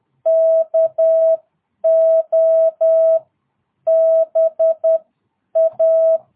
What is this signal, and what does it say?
Morse code, heard as a keyed tone, about 10 wpm, spaced as for 8 wpm, 650 Hz. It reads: KOBA
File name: jh8cbh / 2022cw